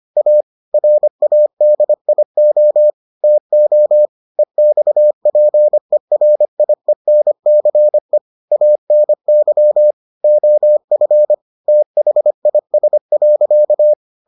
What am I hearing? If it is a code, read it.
Morse code, heard as a keyed tone, about 25 wpm, 600 Hz. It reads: A RADIO TO EXPERIENCE ANY OF THIS.